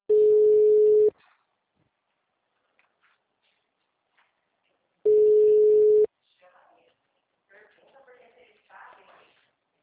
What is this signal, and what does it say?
Ringback tone (Europe, 425 Hz), with speech.